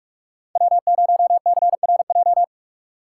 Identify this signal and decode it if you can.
Morse code, heard as a keyed tone, about 42 wpm, 700 Hz. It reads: W0CRJ
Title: USA Call Signs - 5 Characters - 3x Character Spacing - 45wpm